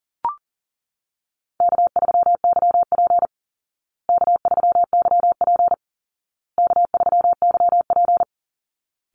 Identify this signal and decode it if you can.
Morse code, heard as a keyed tone, about 37 wpm, 700 Hz. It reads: K3YP K3YP K3YP
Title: Call Signs Repeated 2x - 40wpm